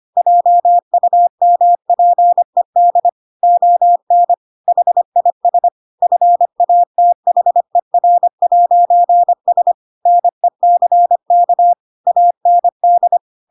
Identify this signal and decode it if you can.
Morse code, heard as a keyed tone, about 25 wpm, 700 Hz. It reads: JUMPED ON HIS FATHER'S NECK AND